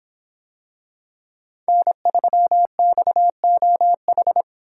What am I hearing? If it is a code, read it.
Morse code, heard as a keyed tone, about 23 wpm, 700 Hz. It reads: N3XOH